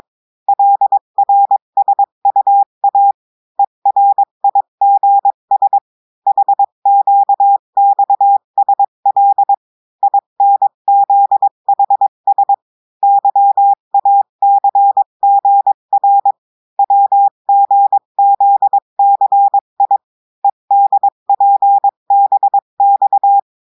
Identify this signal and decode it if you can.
Morse code, heard as a keyed tone, about 22 wpm, 800 Hz. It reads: LRSUA ERIGS HQXSL INZHS YACGR WGZCI EDPBX
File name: cours-du-mercredi / audios / cours-853-du-2-mai-2024-etxte chiffre et texte clair-a-20-wpm